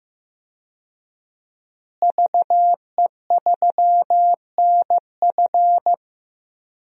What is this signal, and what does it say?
Morse code, heard as a keyed tone, about 15 wpm, 700 Hz. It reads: VE3NF
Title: DXCC-100 Call Signs - 5 Characters - Repeated 2x - 15wpm